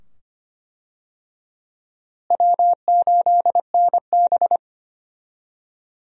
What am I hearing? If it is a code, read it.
Morse code, heard as a keyed tone, about 25 wpm, 700 Hz. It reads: W8NB
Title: USA Call Signs - 4 Characters - Repeated 2x - 25wpm